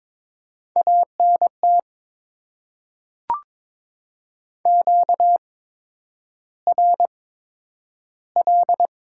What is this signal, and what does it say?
Morse code, heard as a keyed tone, about 22 wpm, spaced as for 9 wpm, 700 Hz. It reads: ANT QRL
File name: Top 20 QSO Elements - 8x Character Spacing - 22wpm